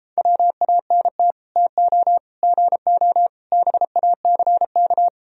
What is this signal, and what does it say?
Morse code, heard as a keyed tone, about 33 wpm, 700 Hz. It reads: WANT TO GO BACK